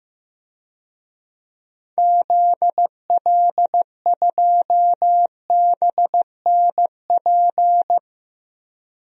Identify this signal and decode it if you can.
Morse code, heard as a keyed tone, about 14 wpm, 700 Hz. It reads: ZL2BNP
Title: DXCC-100 Call Signs - 6 Characters - 15wpm